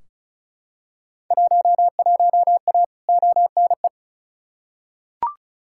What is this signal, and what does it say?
Morse code, heard as a keyed tone, about 36 wpm, 700 Hz. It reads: 11A ONE
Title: Field Day Exchanges Encore - 35wpm